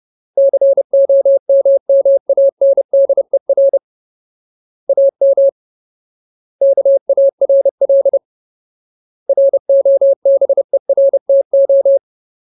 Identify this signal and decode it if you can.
Morse code, heard as a keyed tone, about 30 wpm, 550 Hz. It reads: COMMANDER AM KARL ROBERTO
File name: Ultimate Word List / 30WPM Ultimate